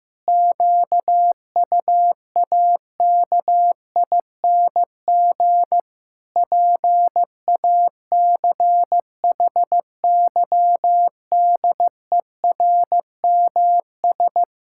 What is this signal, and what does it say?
Morse code, heard as a keyed tone, about 15 wpm, 700 Hz. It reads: QUAKING PACHYDERMS
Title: Pangram Challenge v2 - 15wpm